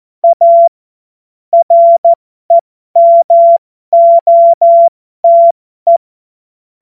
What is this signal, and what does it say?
Morse code, heard as a keyed tone, about 14 wpm, spaced as for 12 wpm, 680 Hz. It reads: A REMOTE
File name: Morsekurs / Witze